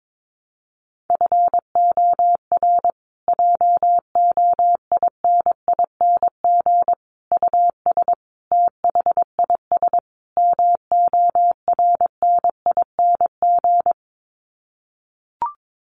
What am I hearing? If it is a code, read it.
Morse code, heard as a keyed tone, about 22 wpm, 700 Hz. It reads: FOR JOINING US THIS MORNING
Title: Sets of 5 Words Encore 22wpm